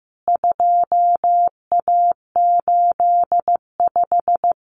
Morse code, heard as a keyed tone, about 15 wpm, 700 Hz. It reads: 2A85
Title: CWT Exchange 15wpm